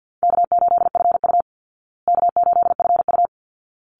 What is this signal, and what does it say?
Morse code, heard as a keyed tone, about 49 wpm, 700 Hz. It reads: K8FU K8FU
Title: Easy Call Signs Repeated 2x - 50wpm